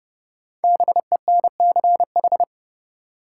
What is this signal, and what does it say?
Morse code, heard as a keyed tone, about 30 wpm, 700 Hz. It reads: BENCH
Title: Sentences from Top 1400 Words - New Word Drill Alt - 30wpm